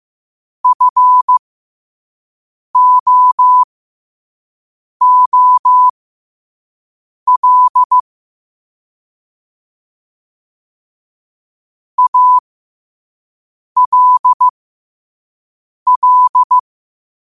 Morse code, heard as a keyed tone, about 15 wpm, spaced as for 5 wpm, 1000 Hz. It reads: FOOL ALL
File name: qotdfiles / 5wpm